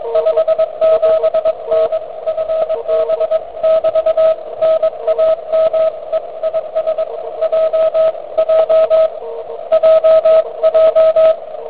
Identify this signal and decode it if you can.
Morse code, heard as a keyed tone, about 22 wpm, 590 Hz, with radio noise. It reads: 57N FB<BT> NAMEIS J J JJ